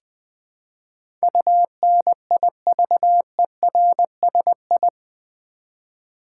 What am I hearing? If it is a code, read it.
Morse code, heard as a keyed tone, about 20 wpm, 700 Hz. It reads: UNIVERSI
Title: Build A Word - 4 Letters At A Time - 10-Letter Words - 20wpm